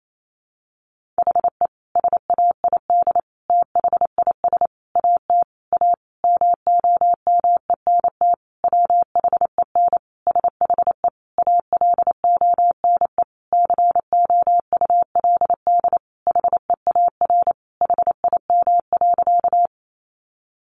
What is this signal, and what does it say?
Morse code, heard as a keyed tone, about 28 wpm, 700 Hz. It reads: HE SAID THIS AT A MOMENT WHEN SHE ALONE COULD HEAR HIM.
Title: Sentences from Top 1000 Words 28wpm